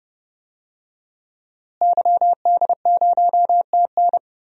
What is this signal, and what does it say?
Morse code, heard as a keyed tone, about 29 wpm, 700 Hz. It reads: YD0TN
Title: DXCC-100 Call Signs - 5 Characters - Repeated 3x - 30wpm